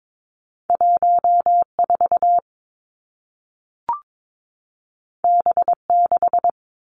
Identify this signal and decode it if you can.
Morse code, heard as a keyed tone, about 22 wpm, 700 Hz. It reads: 1 4 B 6
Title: Two Challenging Letter-Number Combinations 22wpm